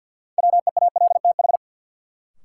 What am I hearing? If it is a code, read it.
Morse code, heard as a keyed tone, about 48 wpm, 700 Hz. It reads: WEALTH
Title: Sentences from Top 1000 Words - Review Part 4 - New Word Drill - 50wpm